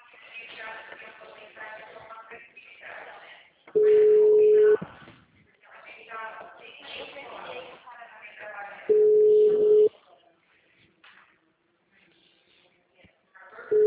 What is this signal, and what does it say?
Ringback tone (Europe, 425 Hz), with speech.
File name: out- -2003- - - .38835